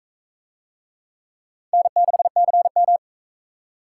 Morse code, heard as a keyed tone, about 39 wpm, 700 Hz. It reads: N6CM